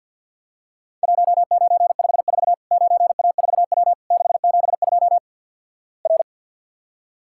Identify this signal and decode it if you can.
Morse code, heard as a keyed tone, about 50 wpm, 700 Hz. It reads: 1954 9A4W 672 R
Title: Worked All Europe DX Contest - QTC Exchange - 50wpm